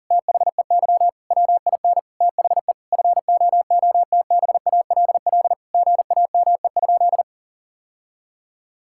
Morse code, heard as a keyed tone, about 40 wpm, 700 Hz. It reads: THEY WIN THE FOOTBALL GAME?
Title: Sentences from Top 1600 Words 40wpm